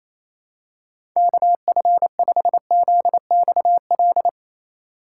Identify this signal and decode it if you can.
Morse code, heard as a keyed tone, about 28 wpm, 700 Hz. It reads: KF5ZXL
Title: USA Call Signs - 6 Characters - 2x Character Spacing - 28wpm